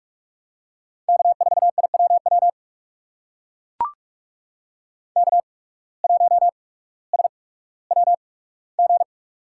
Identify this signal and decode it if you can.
Morse code, heard as a keyed tone, about 44 wpm, spaced as for 19 wpm, 700 Hz. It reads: K4IWW K1SWG
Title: USA Call Signs - 5 Characters - 8x Character Spacing - 45wpm